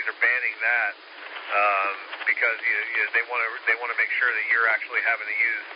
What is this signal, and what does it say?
Emergency Alert System (SAME) tones (header bursts), heard as harsh buzzing bursts, with speech. It is heard from 0.4 to 4.9 s.